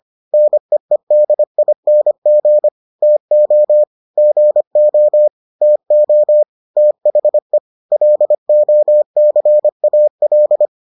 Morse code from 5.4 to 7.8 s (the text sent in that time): TO THE